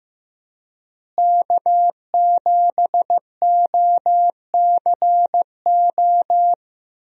Morse code, heard as a keyed tone, about 16 wpm, 700 Hz. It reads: K7OCO
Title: USA Call Signs - 5 Characters - Repeated 3x - 15wpm